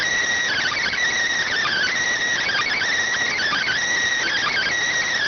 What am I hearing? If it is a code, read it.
Emergency Alert System (SAME) tones (header bursts), heard as harsh buzzing bursts. It is heard from 0.5 to 4.7 s.